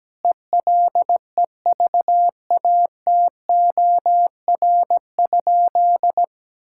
Morse code, heard as a keyed tone, about 17 wpm, 700 Hz. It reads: ELEVATOR?
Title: Sentences from Top 1600 Words 17wpm